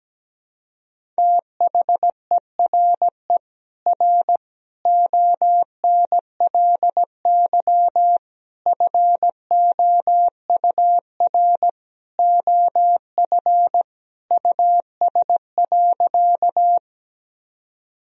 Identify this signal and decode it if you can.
Morse code, heard as a keyed tone, about 17 wpm, 700 Hz. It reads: THERE R ONLY FOUR OF US.
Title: Sentences from Top 300 Words and Abbreviations 17wpm